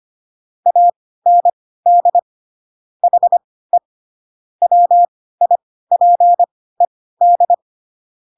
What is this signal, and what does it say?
Morse code, heard as a keyed tone, about 25 wpm, spaced as for 16 wpm, 700 Hz. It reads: AND HE WIPED
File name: Pinocchio / Pinocchio ch3 10 25 ik1hge